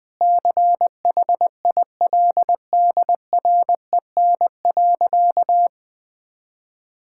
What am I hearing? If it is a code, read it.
Morse code, heard as a keyed tone, about 21 wpm, 700 Hz. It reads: CHILDREN.
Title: Sentences from Top 1000 Words - Review Part 5 - 20wpm